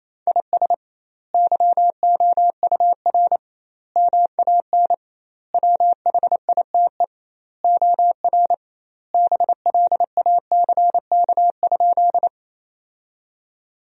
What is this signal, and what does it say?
Morse code, heard as a keyed tone, about 28 wpm, 700 Hz. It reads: IS YOUR MAN WHITE OR BLACK?
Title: 2x Word-Spacing - Sentences from Top 400 Words - 28wpm